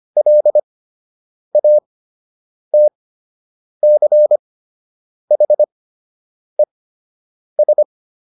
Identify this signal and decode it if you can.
Morse code, heard as a keyed tone, about 25 wpm, spaced as for 8 wpm, 600 Hz. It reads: LATCHES